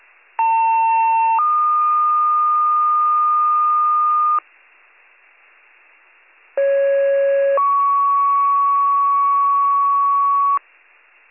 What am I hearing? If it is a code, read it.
Two-tone page, heard as a one-second tone, then a three-second tone, about 900 then 1220 Hz, about 550 then 1120 Hz. It is heard from 0.4 to 10.6 s.